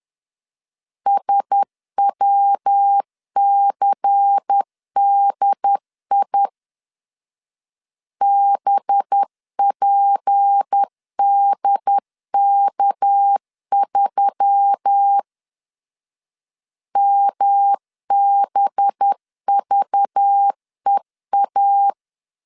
Morse code, heard as a keyed tone, about 10 wpm, 800 Hz. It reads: SWCDI BPDK3 MBVEA
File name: LEZIONI / 6 PKX0 / 6 PKX 10wpm 2426